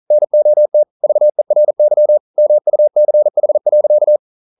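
Morse code, heard as a keyed tone, about 40 wpm, 600 Hz. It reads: NOT VERY MUCH.
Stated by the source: Morse code practice at 41 wpm, from CW med Gnister.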